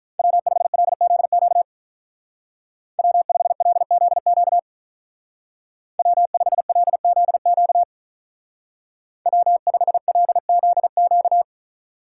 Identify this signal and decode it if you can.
Morse code, heard as a keyed tone, about 43 wpm, 700 Hz. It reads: W5LZQ W5LZQ W5LZQ W5LZQ
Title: Speed Racing - USA 1x3 Call Signs 35wpm